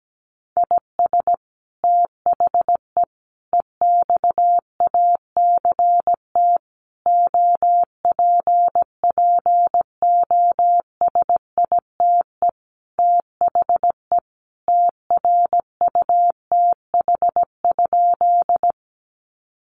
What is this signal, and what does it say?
Morse code, heard as a keyed tone, about 17 wpm, 700 Hz. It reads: IS THE EXACT OPPOSITE THE TRUTH?
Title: Sentences from Top 1300 Words 17wpm